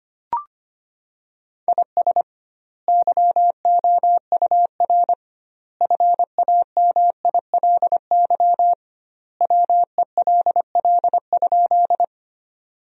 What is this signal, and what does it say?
Morse code, heard as a keyed tone, about 25 wpm, 700 Hz. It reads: IS YOUR FAMILY WELL?
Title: 2x Word-Spacing - Sentences from Top 400 Words - 25wpm